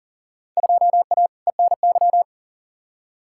Morse code, heard as a keyed tone, about 40 wpm, 700 Hz. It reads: 2A ENY
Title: Field Day Exchanges Encore - 40wpm